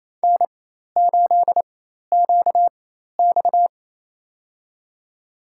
Morse code, heard as a keyed tone, about 28 wpm, spaced as for 13 wpm, 700 Hz. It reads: N8QX
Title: USA Call Signs - 4 Characters - 4x Character Spacing - 28wpm